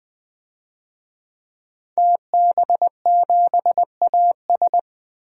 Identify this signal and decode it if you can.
Morse code, heard as a keyed tone, about 20 wpm, 700 Hz. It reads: TB7AS
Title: DXCC-100 Call Signs - 5 Characters - 20wpm